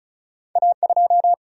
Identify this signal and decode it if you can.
Morse code, heard as a keyed tone, about 36 wpm, 700 Hz. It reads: A2
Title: Single-Letter Single-Number 35wpm